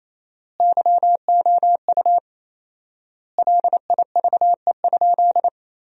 Morse code, heard as a keyed tone, about 28 wpm, 700 Hz. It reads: YOU LIVE?
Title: 4x Word-Spacing - Sentences from Top 200 Words - 28wpm